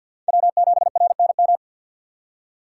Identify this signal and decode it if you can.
Morse code, heard as a keyed tone, about 51 wpm, 700 Hz. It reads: W7RNM
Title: USA Call Signs - 5 Characters - 4x Character Spacing - 50wpm